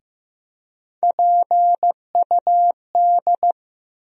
Morse code, heard as a keyed tone, about 15 wpm, 700 Hz. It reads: PUD
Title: USA Call Sign Postfixes - 3 Letters 15wpm